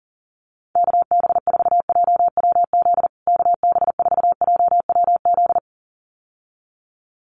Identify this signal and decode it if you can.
Morse code, heard as a keyed tone, about 40 wpm, 700 Hz. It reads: KB4JWZ KB4JWZ
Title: USA Call Signs Repeated 2x 40wpm